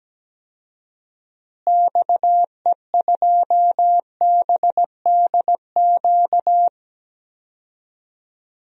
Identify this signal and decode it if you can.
Morse code, heard as a keyed tone, about 17 wpm, 700 Hz. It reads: XE2BDQ
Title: DXCC-100 Call Signs - 6 Characters - Repeated 2x - 17wpm